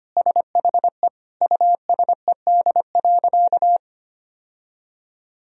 Morse code, heard as a keyed tone, about 25 wpm, 700 Hz. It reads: SHE USED.
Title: Sentences from Top 1000 Words - Review Part 1 - 25wpm